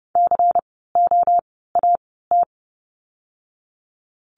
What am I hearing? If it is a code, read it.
Morse code, heard as a keyed tone, about 30 wpm, 700 Hz. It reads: C O A T
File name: Lesson 16f - Letter c - Words 30wpm